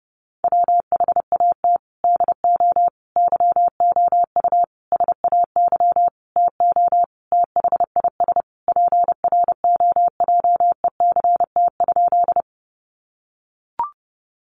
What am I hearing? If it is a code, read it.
Morse code, heard as a keyed tone, about 30 wpm, 700 Hz. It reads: WHAT DO YOU SAY TO THIS PROJECT?
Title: Sentences from Top 1400 Words 30wpm